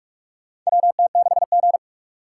Morse code, heard as a keyed tone, about 44 wpm, 700 Hz. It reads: WT6G
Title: USA Call Signs - 4 Characters - 4x Character Spacing - 45wpm